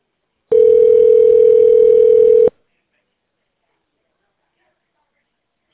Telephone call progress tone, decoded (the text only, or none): ringback tone